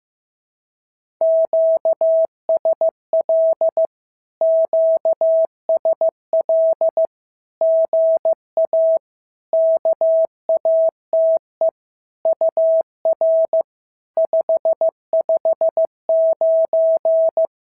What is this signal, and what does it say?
Morse code, heard as a keyed tone, about 15 wpm, 650 Hz. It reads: QSL QSL GA KATE UR 559
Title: SOTA QSO Practice 15wpm